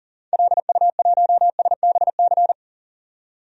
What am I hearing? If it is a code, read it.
Morse code, heard as a keyed tone, about 38 wpm, 700 Hz. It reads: LU1SBC
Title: DXCC-100 Call Signs - 6 Characters - 40wpm